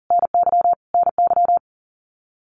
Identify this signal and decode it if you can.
Morse code, heard as a keyed tone, about 40 wpm, 700 Hz. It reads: NY NY
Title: Common QSO Phrases Encore 40wpm